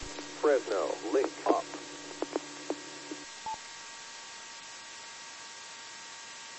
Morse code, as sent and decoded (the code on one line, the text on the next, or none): none